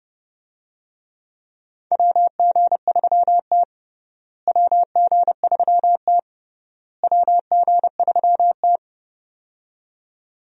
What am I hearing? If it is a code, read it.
Morse code, heard as a keyed tone, about 30 wpm, 700 Hz. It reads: WG3T WG3T WG3T
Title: USA Call Signs - 4 Characters - Repeated 3x - 30wpm